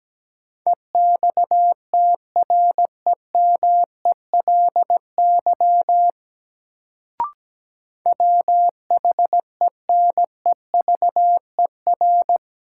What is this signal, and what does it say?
Morse code, heard as a keyed tone, about 17 wpm, 700 Hz. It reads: EXTREMELY WHENEVER
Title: Sentences from Top 1200 Words - New Word Drill Alt - 17wpm